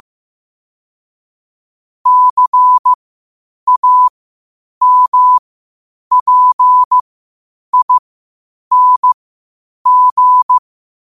Morse code, heard as a keyed tone, about 15 wpm, spaced as for 8 wpm, 1000 Hz. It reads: CAMPING
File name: qotdfiles / 8wpm